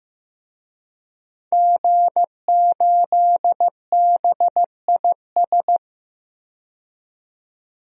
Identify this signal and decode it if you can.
Morse code, heard as a keyed tone, about 15 wpm, 700 Hz. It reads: G8BIS
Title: DXCC-100 Call Signs - 5 Characters - 15wpm